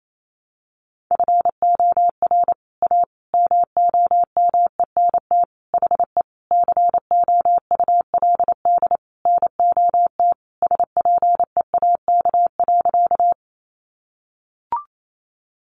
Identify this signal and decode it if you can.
Morse code, heard as a keyed tone, about 28 wpm, 700 Hz. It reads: FOR A MOMENT HE COULD NOT SPEAK.
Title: Sentences from Top 1100 Words 28wpm